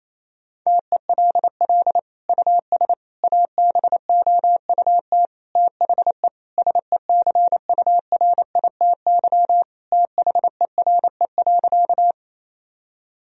Morse code, heard as a keyed tone, about 28 wpm, 700 Hz. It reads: TELL US ABOUT THE SECURITY THERE.
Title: Sentences from Top 1000 Words - Review Part 4 - 28wpm